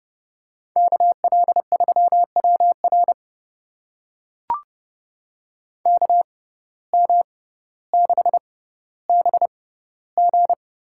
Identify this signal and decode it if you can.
Morse code, heard as a keyed tone, about 30 wpm, spaced as for 15 wpm, 700 Hz. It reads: KL3WR KM6BG
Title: USA Call Signs - 6x Character Spacing - 30wpm